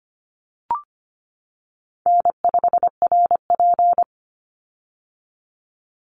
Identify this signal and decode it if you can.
Morse code, heard as a keyed tone, about 25 wpm, 700 Hz. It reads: N5RP
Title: Easy Call Signs - 25wpm